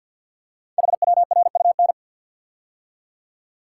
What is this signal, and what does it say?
Morse code, heard as a keyed tone, about 46 wpm, 700 Hz. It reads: HWRUN
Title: Code Groups - 5 Letters - 50wpm